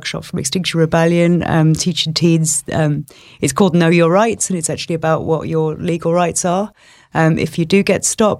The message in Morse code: none

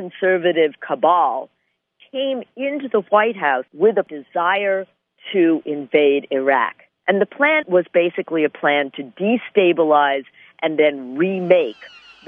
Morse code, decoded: none

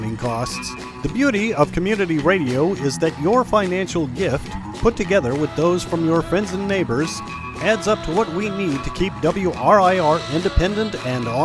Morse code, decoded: none

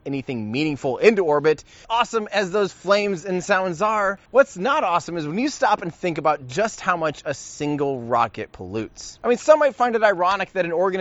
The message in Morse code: none

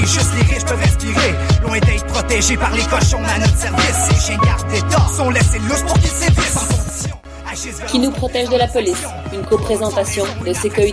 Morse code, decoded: none

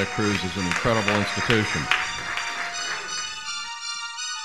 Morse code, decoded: none

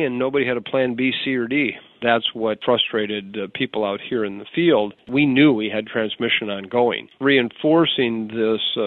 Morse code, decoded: none

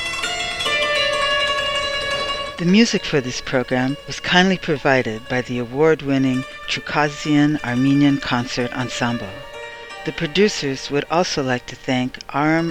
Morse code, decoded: none

